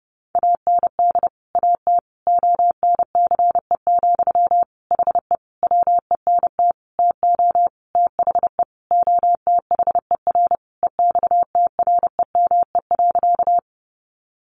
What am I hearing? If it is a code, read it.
Morse code, heard as a keyed tone, about 30 wpm, 700 Hz. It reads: AND AT ONCE, HE WENT TO THE OTHER EXTREME.
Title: Sentences from Top 1300 Words 30wpm